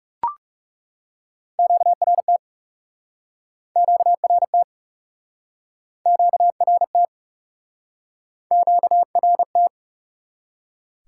Morse code, heard as a keyed tone, about 42 wpm, 700 Hz. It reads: QRT QRT QRT QRT